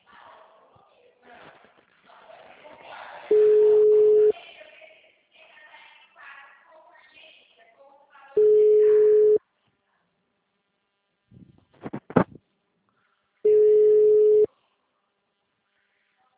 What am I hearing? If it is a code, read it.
Ringback tone (Europe, 425 Hz), with speech.
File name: out- -2010- - - .9453